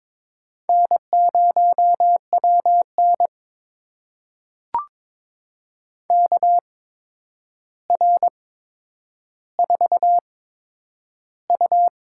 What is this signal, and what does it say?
Morse code, heard as a keyed tone, about 22 wpm, spaced as for 9 wpm, 700 Hz. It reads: N0WN KR4U
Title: USA Call Signs - 4 Characters - 8x Character Spacing - 22wpm